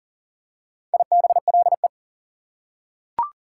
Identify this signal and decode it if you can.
Morse code, heard as a keyed tone, about 40 wpm, 700 Hz. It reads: IBLE
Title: Prefixes and Suffixes 40wpm